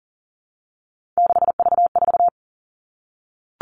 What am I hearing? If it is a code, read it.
Morse code, heard as a keyed tone, about 38 wpm, 700 Hz. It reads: 6V4
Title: Three Challenging Letter-Number Combinations 40wpm